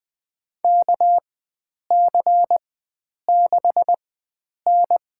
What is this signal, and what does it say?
Morse code, heard as a keyed tone, about 20 wpm, spaced as for 9 wpm, 700 Hz. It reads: KC6N
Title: USA Call Signs - 4 Characters - 4x Character Spacing - 20wpm